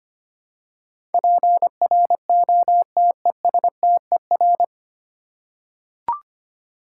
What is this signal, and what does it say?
Morse code, heard as a keyed tone, about 25 wpm, 700 Hz. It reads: PROTESTER
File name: Nine Letter Words - 25wpm